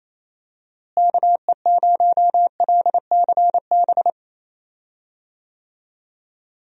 Morse code, heard as a keyed tone, about 28 wpm, 700 Hz. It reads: KE0LCB
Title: USA Call Signs - 6 Characters - Repeated 3x - 28wpm